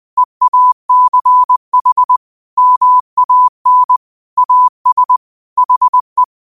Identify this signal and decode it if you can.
Morse code, heard as a keyed tone, about 20 wpm, 1000 Hz. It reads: EACH MAN AS HE
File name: qotdfiles / 20wpm